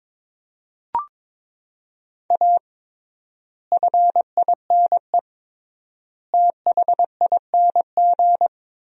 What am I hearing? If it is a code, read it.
Morse code, heard as a keyed tone, about 22 wpm, 700 Hz. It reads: A FINE THING
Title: 3x Word-Spacing - Sentences from Top 500 Words - 22wpm